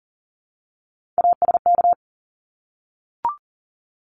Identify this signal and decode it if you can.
Morse code, heard as a keyed tone, about 40 wpm, 700 Hz. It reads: ASK
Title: ED Suffix - Top 100 Words with Base Words 40wpm